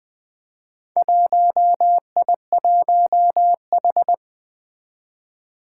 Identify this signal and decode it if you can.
Morse code, heard as a keyed tone, about 20 wpm, 700 Hz. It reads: 1I1H
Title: Four Challenging Letter-Number Combinations 20wpm